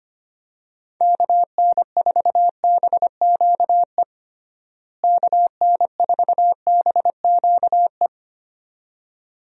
Morse code, heard as a keyed tone, about 25 wpm, 700 Hz. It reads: KN4BQE KN4BQE
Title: USA Call Signs - 6 Characters - Repeated 2x - 25wpm